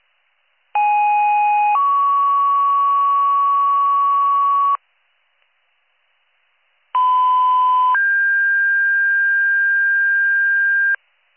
Two-tone page, heard as a one-second tone, then a three-second tone, about 830 then 1120 Hz, about 1010 then 1600 Hz. It is heard from 0.7 to 11.0 s.